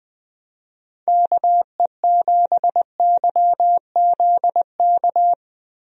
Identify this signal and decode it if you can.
Morse code, heard as a keyed tone, about 21 wpm, 700 Hz. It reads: KE7YZK
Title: USA Call Signs - 6 Characters - Repeated 2x - 20wpm